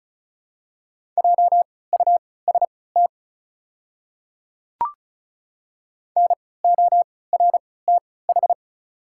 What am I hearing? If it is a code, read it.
Morse code, heard as a keyed tone, about 36 wpm, spaced as for 20 wpm, 700 Hz. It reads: JUST NORTH
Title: 3x Character Spacing - Top 500 Words - 35wpm